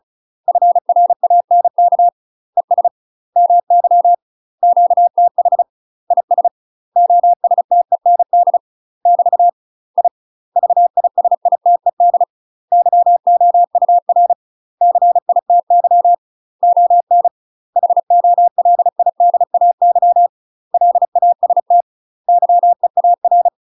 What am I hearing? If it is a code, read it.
Morse code, heard as a keyed tone, about 36 wpm, 700 Hz. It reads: FRANK ES MY QTH IS OSTEND <BT> I VISITED YOUR CITY ON HOLIDAY LAST YEAR
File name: UK MASTER QSOs TRAINING / 35WPM / UK MASTER QSOs 0040